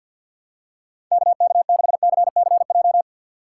Morse code, heard as a keyed tone, about 52 wpm, 700 Hz. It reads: KK6CCJ
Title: USA Call Signs - 4x Character Spacing - 50wpm